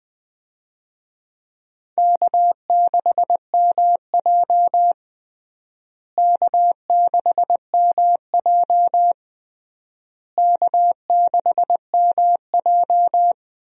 Morse code, heard as a keyed tone, about 20 wpm, 700 Hz. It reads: K6MJ K6MJ K6MJ
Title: USA Call Signs - 4 Characters - Repeated 3x - 20wpm